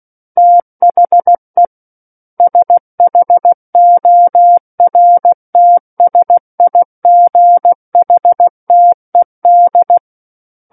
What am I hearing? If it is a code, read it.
Morse code, heard as a keyed tone, about 16 wpm, 700 Hz. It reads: THE SHORTSIGHTED
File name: Book One / 07-Chapter 7 - 16 WPM